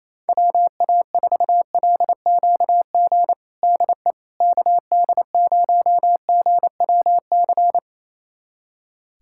Morse code, heard as a keyed tone, about 27 wpm, 700 Hz. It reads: WA4LQG DE KD0GWC
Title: USA Callsign DE Callsign - 6 Characters - 28wpm